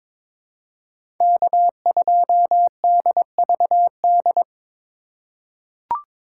Morse code, heard as a keyed tone, about 21 wpm, 700 Hz. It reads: K2DVD
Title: USA Call Signs - 5 Characters - 2x Character Spacing - 22wpm